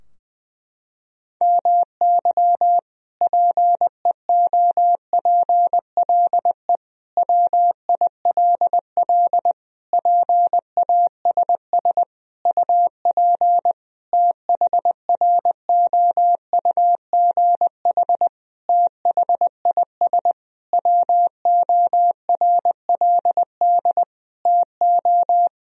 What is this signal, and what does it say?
Morse code, heard as a keyed tone, about 20 wpm, 700 Hz. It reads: MY PEOPLE WILL PASS UP THROUGH THIS WORLD TO
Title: Sentences from Top 1000 Words - Review Part 2 - 20wpm